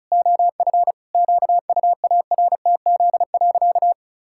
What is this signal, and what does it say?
Morse code, heard as a keyed tone, about 34 wpm, 700 Hz. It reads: OF QUARTZ.
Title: Pangram Challenge v2 - 35wpm